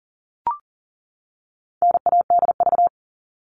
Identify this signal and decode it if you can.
Morse code, heard as a keyed tone, about 40 wpm, 700 Hz. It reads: NADV
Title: All Cut Numbers - 4-Digit 40wpm